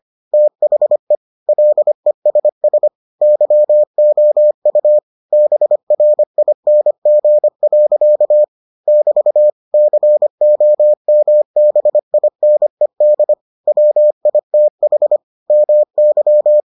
Morse code, heard as a keyed tone, about 25 wpm, 600 Hz. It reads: THE LESS YOU BRING. <BT> COMBINED WITH MY